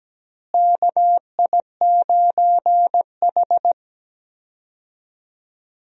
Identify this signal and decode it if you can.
Morse code, heard as a keyed tone, about 17 wpm, 700 Hz. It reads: KI9H
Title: USA Call Signs - 4 Characters - Repeated 2x - 17wpm